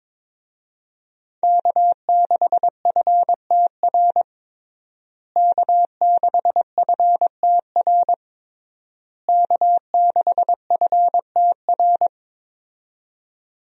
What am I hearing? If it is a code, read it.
Morse code, heard as a keyed tone, about 22 wpm, 700 Hz. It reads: K6FTR K6FTR K6FTR
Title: USA Call Signs - 5 Characters - Repeated 3x - 22wpm